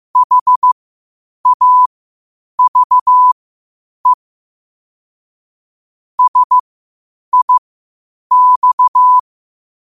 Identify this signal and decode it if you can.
Morse code, heard as a keyed tone, about 15 wpm, spaced as for 8 wpm, 1000 Hz. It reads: HAVE SIX
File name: qotdfiles / 8wpm